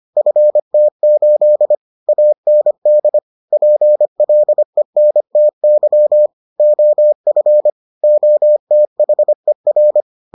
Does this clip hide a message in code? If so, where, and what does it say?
Yes: Morse code from 0.2 to 10.0 s, heard as a keyed tone, about 25 wpm, 600 Hz. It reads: FT8 AND PLENTY OF OTHER